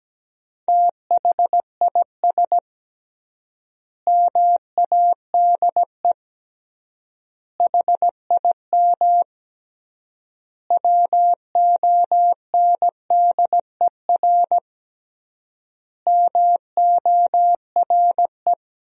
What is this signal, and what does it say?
Morse code, heard as a keyed tone, about 17 wpm, 700 Hz. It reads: THIS MADE HIM WONDER MORE
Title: 3x Word-Spacing - Sentences from Top 500 Words - 17wpm